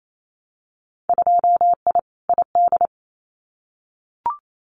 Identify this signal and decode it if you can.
Morse code, heard as a keyed tone, about 28 wpm, 700 Hz. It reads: 2I ID
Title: Winter Field Day Exchanges Encore 28wpm